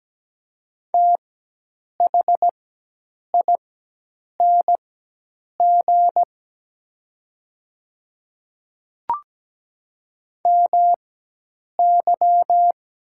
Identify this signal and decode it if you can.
Morse code, heard as a keyed tone, about 17 wpm, spaced as for 8 wpm, 700 Hz. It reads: THING MY
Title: 4x Character Spacing - Top 100 Words - 17wpm